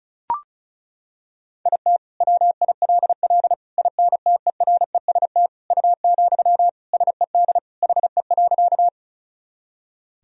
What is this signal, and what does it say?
Morse code, heard as a keyed tone, about 36 wpm, 700 Hz. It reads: IT WILL INTEREST U, SED HE.
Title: Sentences from Top 400 Words and Abbreviations 35wpm